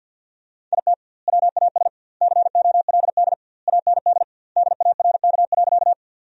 Morse code, heard as a keyed tone, about 50 wpm, 700 Hz. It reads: IT WAS COLD AND DARK.
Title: 2x Word-Spacing - Sentences from Top 500 Words - 50wpm